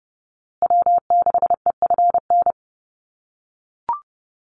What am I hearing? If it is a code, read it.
Morse code, heard as a keyed tone, about 30 wpm, 700 Hz. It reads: W6EFN
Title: USA Call Signs Repeated 3x 30wpm